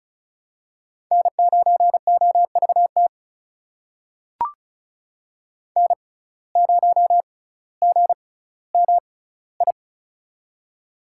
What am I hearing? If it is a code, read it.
Morse code, heard as a keyed tone, about 36 wpm, spaced as for 18 wpm, 700 Hz. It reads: N9OVT N0GMI
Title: USA Call Signs - 5 Characters - 6x Character Spacing - 35wpm